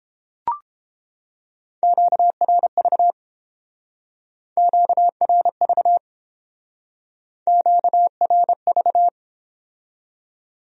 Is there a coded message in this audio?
Yes: Morse code from 1.8 to 9.1 s, heard as a keyed tone, about 30 wpm, 700 Hz. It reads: QRV QRV QRV